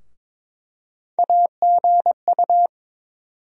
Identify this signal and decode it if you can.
Morse code, heard as a keyed tone, about 21 wpm, 700 Hz. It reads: AGU
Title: USA Call Sign Postfixes - 3 Letters 22wpm